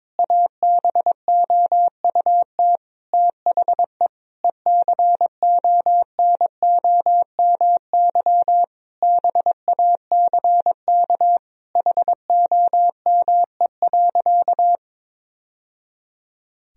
Morse code, heard as a keyed tone, about 22 wpm, 700 Hz. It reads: ABOUT THE ECONOMY BACK HOME.
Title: Sentences from Top 1000 Words - Review Part 3 - 22wpm